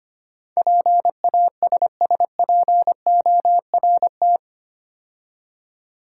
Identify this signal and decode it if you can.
Morse code, heard as a keyed tone, about 25 wpm, 700 Hz. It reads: PASSPORT
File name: 8 Letter Words - First 3 Letters Given - 25wpm